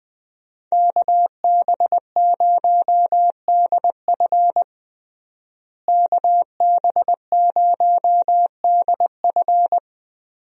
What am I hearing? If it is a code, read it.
Morse code, heard as a keyed tone, about 20 wpm, 700 Hz. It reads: KB0DF KB0DF
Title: USA Call Signs - 5 Characters - Repeated 2x - 20wpm